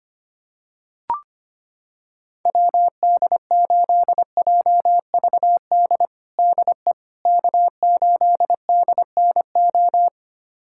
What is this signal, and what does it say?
Morse code, heard as a keyed tone, about 25 wpm, 700 Hz. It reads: WD8JVD DE K8DNO
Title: USA Callsign DE Callsign 25wpm